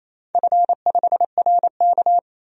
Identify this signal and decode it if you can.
Morse code, heard as a keyed tone, about 28 wpm, 700 Hz. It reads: F5RK
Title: DXCC-100 Call Signs - 4 Characters - 28wpm